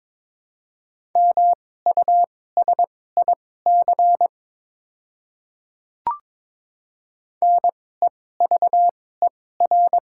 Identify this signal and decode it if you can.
Morse code, heard as a keyed tone, about 22 wpm, spaced as for 16 wpm, 700 Hz. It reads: MUSIC NEVER
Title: 2x Character Spacing - Top 300 Words - 22wpm